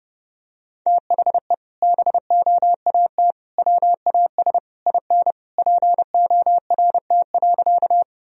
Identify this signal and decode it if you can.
Morse code, heard as a keyed tone, about 30 wpm, 700 Hz. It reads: THE BOAT WAS IN PORT.
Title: Sentences from Top 1000 Words - Review Part 1 - 30wpm